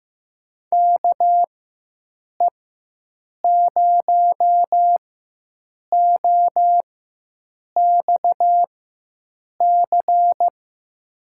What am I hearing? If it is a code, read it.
Morse code, heard as a keyed tone, about 15 wpm, spaced as for 7 wpm, 700 Hz. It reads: KE0OXC
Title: USA Call Signs - 6 Characters - 4x Character Spacing - 15wpm